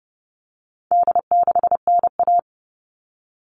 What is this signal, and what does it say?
Morse code, heard as a keyed tone, about 30 wpm, 700 Hz. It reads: D6NA
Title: All Cut Numbers - 4-Digit 30wpm